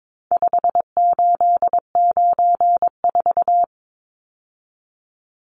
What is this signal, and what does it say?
Morse code, heard as a keyed tone, about 22 wpm, 700 Hz. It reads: 5894
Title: Numbers - 4-Digit 22wpm